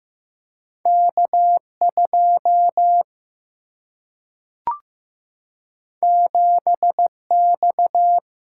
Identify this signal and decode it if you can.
Morse code, heard as a keyed tone, about 15 wpm, 700 Hz. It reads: K 2 7 X
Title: Single-Letter Single-Number 15wpm